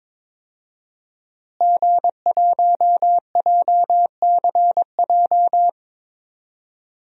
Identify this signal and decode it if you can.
Morse code, heard as a keyed tone, about 22 wpm, 700 Hz. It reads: G1JCJ
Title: DXCC-100 Call Signs - 5 Characters - Repeated 3x - 22wpm